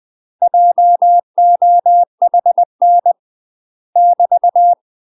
Morse code, heard as a keyed tone, about 20 wpm, 700 Hz. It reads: JOHN <BT>
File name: UK MASTER QSOs TRAINING / 20WPM / UK MASTER QSOs 0014